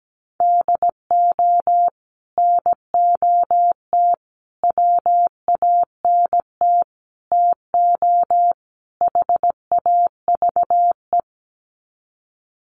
Morse code, heard as a keyed tone, about 17 wpm, 700 Hz. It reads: DO NOT WANT TO HAVE
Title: Sets of 5 Words 17wpm